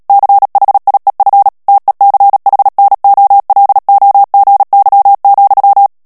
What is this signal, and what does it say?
Morse code, heard as a keyed tone, about 37 wpm, 800 Hz. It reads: CHIEF TECHNOLOGY,
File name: morse mp3 / ANGLAIS texte clair 3 vitesse 35 mots minute